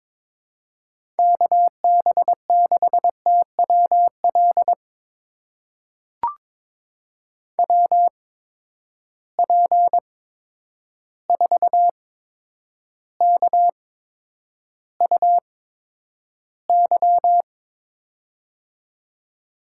Morse code, heard as a keyed tone, about 22 wpm, spaced as for 9 wpm, 700 Hz. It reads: KB6TWL WP4KUY
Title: USA Call Signs - 6 Characters - 8x Character Spacing - 22wpm